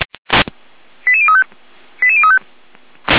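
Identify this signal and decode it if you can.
Five-tone selective call (ZVEI), heard as quick runs of five tones, tones of about 80 ms. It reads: 88035 88035
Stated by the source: ZVEI five-tone selective call 88074 received via SDR.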